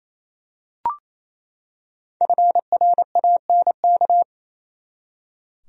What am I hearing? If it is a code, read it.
Morse code, heard as a keyed tone, about 28 wpm, 700 Hz. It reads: FRANK
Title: USA Common Names - 28wpm